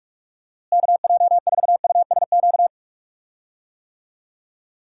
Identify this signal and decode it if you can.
Morse code, heard as a keyed tone, about 42 wpm, 700 Hz. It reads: KJ4USQ
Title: USA Call Signs - 6 Characters - Repeated 2x - 45wpm